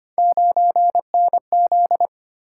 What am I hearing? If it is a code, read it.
Morse code, heard as a keyed tone, about 25 wpm, 700 Hz. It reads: 9NZ
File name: Three Letter-Number Combinations - 25wpm